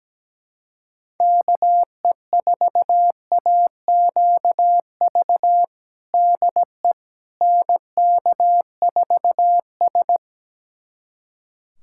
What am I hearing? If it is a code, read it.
Morse code, heard as a keyed tone, about 17 wpm, 700 Hz. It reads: KE4AQV DE NK4S
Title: USA Callsign DE Callsign 17wpm